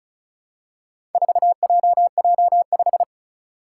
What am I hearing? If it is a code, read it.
Morse code, heard as a keyed tone, about 36 wpm, 700 Hz. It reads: 4JJ5
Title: Four Challenging Letter-Number Combinations 35wpm